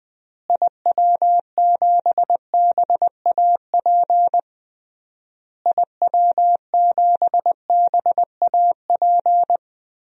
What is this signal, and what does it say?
Morse code, heard as a keyed tone, about 20 wpm, 700 Hz. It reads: IW7BAP IW7BAP
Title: DXCC-100 Call Signs - 6 Characters - Repeated 3x - 20wpm